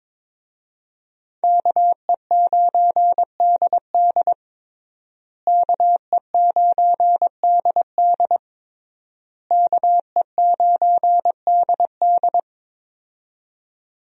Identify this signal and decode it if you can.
Morse code, heard as a keyed tone, about 22 wpm, 700 Hz. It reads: KE9DD KE9DD KE9DD
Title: USA Call Signs - 5 Characters - Repeated 3x - 22wpm